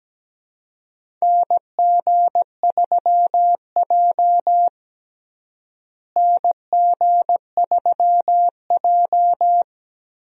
Morse code, heard as a keyed tone, about 17 wpm, 700 Hz. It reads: NG3J NG3J
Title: USA Call Signs - 4 Characters - Repeated 2x - 17wpm